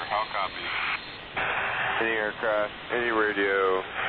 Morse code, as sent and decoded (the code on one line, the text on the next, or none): none